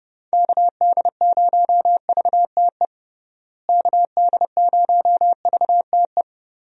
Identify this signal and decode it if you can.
Morse code, heard as a keyed tone, about 30 wpm, 700 Hz. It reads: KD0VTE KD0VTE
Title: USA Call Signs - 6 Characters - Repeated 3x - 30wpm